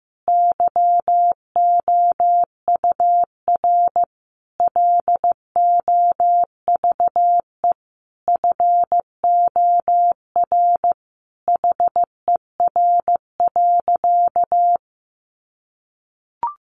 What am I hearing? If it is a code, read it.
Morse code, heard as a keyed tone, about 14 wpm, 700 Hz. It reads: YOUR LOVE FOR HER.
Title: Sentences from Top 1400 Words 15wpm